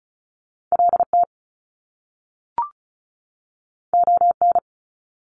Morse code, heard as a keyed tone, about 36 wpm, 700 Hz. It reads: LT ON
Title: Top Two Letter Combinations in Words 35wpm